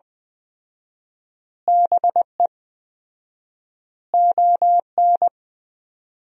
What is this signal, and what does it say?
Morse code, heard as a keyed tone, about 20 wpm, 700 Hz. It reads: BE ON
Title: 4x Word-Spacing - Sentences from Top 300 Words - 20wpm